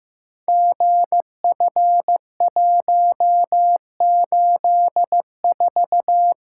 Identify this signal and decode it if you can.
Morse code, heard as a keyed tone, about 15 wpm, 700 Hz. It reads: GF184
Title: Code Groups - 5 Letter:Numbers - 15wpm